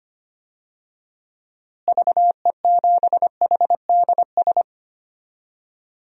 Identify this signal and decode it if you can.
Morse code, heard as a keyed tone, about 25 wpm, 700 Hz. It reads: VE7HDS
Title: DXCC-100 Call Signs - 6 Characters - Repeated 2x - 25wpm